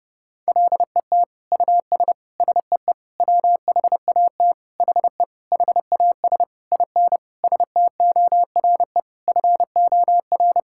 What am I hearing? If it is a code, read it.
Morse code, heard as a keyed tone, about 30 wpm, 700 Hz. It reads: LET US SEE WHAT HE HAS IN STORE FOR
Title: Sentences from Top 1000 Words - Review Part 5 - 30wpm